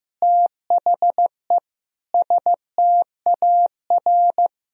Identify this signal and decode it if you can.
Morse code, heard as a keyed tone, about 16 wpm, 700 Hz. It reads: THE STAR
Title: Sentences from Top 1000 Words - Review Part 3 - 15wpm